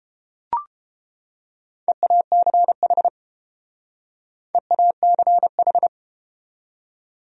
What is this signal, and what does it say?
Morse code, heard as a keyed tone, about 29 wpm, 700 Hz. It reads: EACH EACH